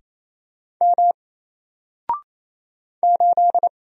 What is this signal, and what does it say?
Morse code, heard as a keyed tone, about 27 wpm, 700 Hz. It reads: M 8
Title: Single Character (Letter:Number:Punctuation:Prosign) Alt 28wpm